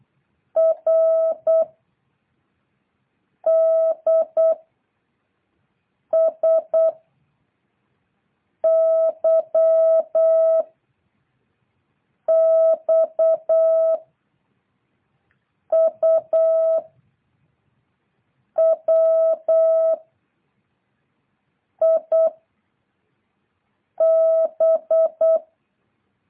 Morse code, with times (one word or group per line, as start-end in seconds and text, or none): 0.6-1.6 R
3.5-4.5 D
6.1-6.9 S
8.6-10.6 Y
12.3-14.0 X
15.7-16.8 U
18.6-19.9 W
21.8-22.3 I
24.0-25.4 B